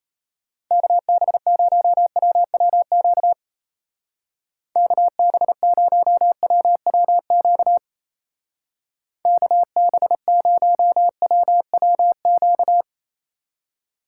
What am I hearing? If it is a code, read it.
Morse code, heard as a keyed tone, about 34 wpm, 700 Hz. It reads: KB0WWQ KB0WWQ KB0WWQ
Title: Speed Racing - USA Call Signs 28wpm